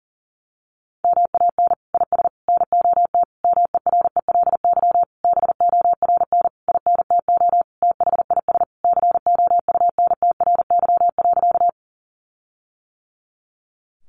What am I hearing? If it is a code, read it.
Morse code, heard as a keyed tone, about 40 wpm, 700 Hz. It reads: MAN IS NOT MERELY BORN INTO THIS COUNTRY.
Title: Sentences from Top 1400 Words 40wpm